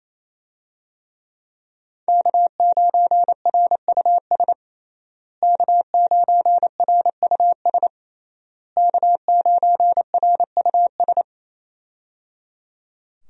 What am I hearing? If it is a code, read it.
Morse code, heard as a keyed tone, about 28 wpm, 700 Hz. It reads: K9RUS K9RUS K9RUS
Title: USA Call Signs - 5 Characters - Repeated 3x - 28wpm